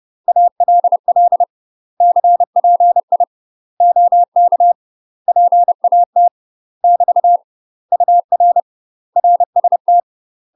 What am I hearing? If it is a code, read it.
Morse code, heard as a keyed tone, about 30 wpm, 700 Hz. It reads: ALL CPI OK PAT <BT> UR RST
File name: UK MASTER QSOs TRAINING / 30WPM / UK MASTER QSOs 0033